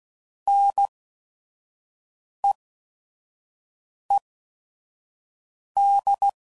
Morse code, heard as a keyed tone, about 16 wpm, spaced as for 5 wpm, 800 Hz. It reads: NEED
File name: Prefixes / By First Letter / Ns